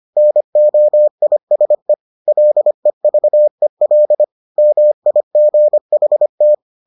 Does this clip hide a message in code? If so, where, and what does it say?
Yes: Morse code from 0.2 to 6.5 s, heard as a keyed tone, about 25 wpm, 600 Hz. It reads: NOISE LEVEL MIGHT